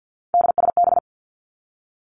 Morse code, heard as a keyed tone, about 50 wpm, 700 Hz. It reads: DSB
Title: Three Challenging Letter-Number Combinations 50wpm